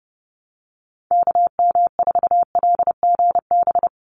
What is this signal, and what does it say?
Morse code, heard as a keyed tone, about 30 wpm, 700 Hz. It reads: KM4LGB
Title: USA Call Signs 30wpm